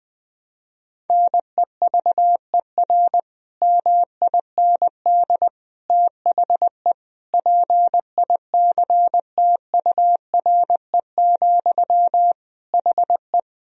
Morse code, heard as a keyed tone, about 20 wpm, 700 Hz. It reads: NEVER MIND THE PICTURE, HE
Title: Sentences from Top 1000 Words - Review Part 1 - 20wpm